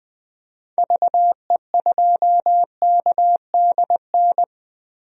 Morse code, heard as a keyed tone, about 21 wpm, 700 Hz. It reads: VE2KDN
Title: DXCC-100 Call Signs - 6 Characters - Repeated 3x - 20wpm